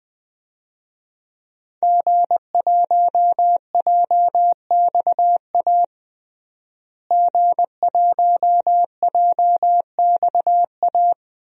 Morse code, heard as a keyed tone, about 20 wpm, 700 Hz. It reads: G1JXA G1JXA
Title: DXCC-100 Call Signs - 5 Characters - Repeated 2x - 20wpm